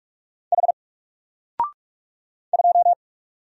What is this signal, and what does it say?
Morse code, heard as a keyed tone, about 44 wpm, 700 Hz. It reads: H 2
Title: Single Letter-Number Alt 45wpm